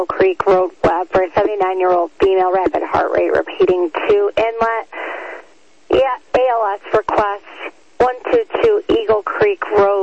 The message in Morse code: none

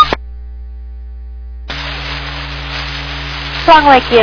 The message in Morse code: none

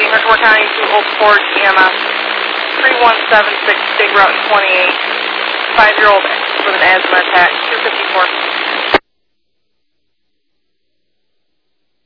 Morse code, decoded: none

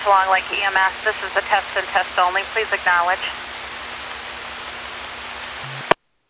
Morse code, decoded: none

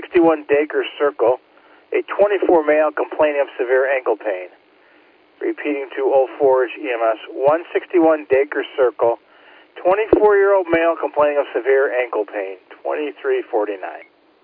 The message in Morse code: none

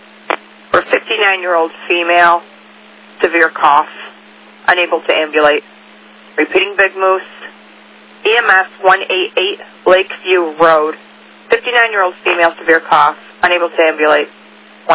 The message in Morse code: none

